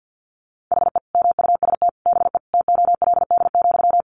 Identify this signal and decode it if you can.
Morse code, heard as a keyed tone, about 50 wpm, 700 Hz. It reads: HE MUST BE TOLD,
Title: Sentences from Top 1400 Words 50wpm